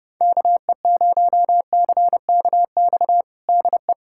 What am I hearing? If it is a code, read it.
Morse code, heard as a keyed tone, about 30 wpm, 700 Hz. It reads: KE0CKX DE
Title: USA Callsign DE Callsign - 6 Characters - 30wpm